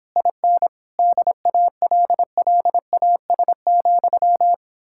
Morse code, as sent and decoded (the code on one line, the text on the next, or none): .. -. / -.. .- .-.. .-.. .- ... --..--
IN DALLAS,